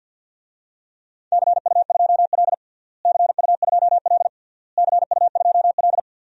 Morse code, heard as a keyed tone, about 53 wpm, 700 Hz. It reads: CU2L CU2L CU2L
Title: DXCC-100 Call Signs - 4 Characters - Repeated 3x - 50wpm